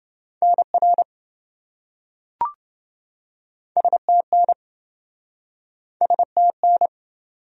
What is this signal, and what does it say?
Morse code, heard as a keyed tone, about 26 wpm, 700 Hz. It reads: NR STN STN